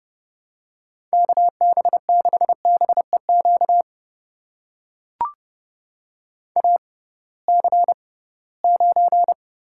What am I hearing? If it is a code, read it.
Morse code, heard as a keyed tone, about 30 wpm, spaced as for 19 wpm, 700 Hz. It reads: KB6BEQ AC9
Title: USA Call Signs - 6x Character Spacing - 30wpm